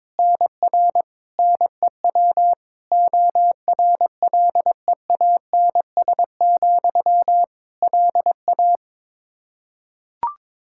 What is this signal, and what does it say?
Morse code, heard as a keyed tone, about 22 wpm, 700 Hz. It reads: NR NEW ORLEANS, LA
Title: K1USN SST QSO Practice 22wpm